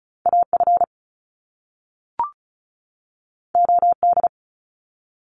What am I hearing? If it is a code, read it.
Morse code, heard as a keyed tone, about 36 wpm, 700 Hz. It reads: A F O D
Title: Top Two Letter Combinations in Words 35wpm